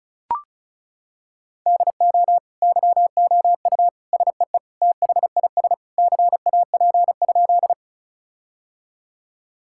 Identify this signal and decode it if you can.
Morse code, heard as a keyed tone, about 36 wpm, 700 Hz. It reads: DO YOU SEE THIS CAP?
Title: Sentences from Top 1600 Words 35wpm